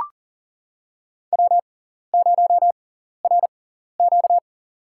Morse code, heard as a keyed tone, about 40 wpm, spaced as for 14 wpm, 700 Hz. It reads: W0RQ
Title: USA Call Signs - 6x Character Spacing - 40wpm